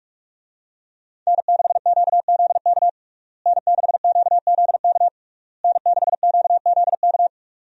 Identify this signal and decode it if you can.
Morse code, heard as a keyed tone, about 47 wpm, 700 Hz. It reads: N6QZK N6QZK N6QZK
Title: USA Call Signs - 5 Characters - Repeated 3x - 45wpm